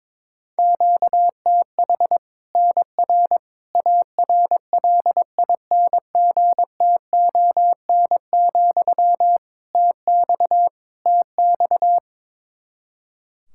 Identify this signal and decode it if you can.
Morse code, heard as a keyed tone, about 22 wpm, 700 Hz. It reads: QTH NR ARLINGTON, TX TX
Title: Practice QSO - Exchange 1 - Beginning - 22wpm